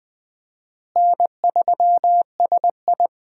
Morse code, heard as a keyed tone, about 21 wpm, 700 Hz. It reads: N3SI
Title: USA Call Signs - 4 Characters - Repeated 2x - 20wpm